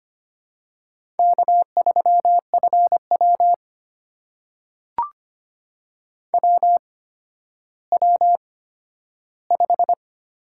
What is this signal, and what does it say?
Morse code, heard as a keyed tone, about 25 wpm, spaced as for 12 wpm, 700 Hz. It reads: K3FW WW5
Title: USA Call Signs - 4 Characters - 8x Character Spacing - 25wpm